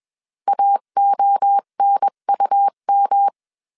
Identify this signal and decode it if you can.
Morse code, heard as a keyed tone, about 21 wpm, 800 Hz. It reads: AONUM
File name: 20wpm